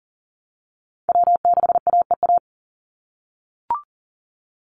Morse code, heard as a keyed tone, about 40 wpm, 700 Hz. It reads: W6AEA
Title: Call Signs - 40wpm